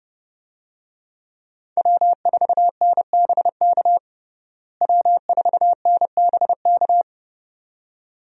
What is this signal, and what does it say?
Morse code, heard as a keyed tone, about 30 wpm, 700 Hz. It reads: W4NBK W4NBK
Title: USA Call Signs - 5 Characters - Repeated 2x - 30wpm